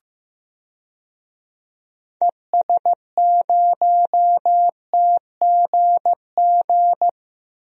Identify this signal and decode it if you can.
Morse code, heard as a keyed tone, about 15 wpm, 700 Hz. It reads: ES0TGG
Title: DXCC-100 Call Signs - 6 Characters - Repeated 3x - 15wpm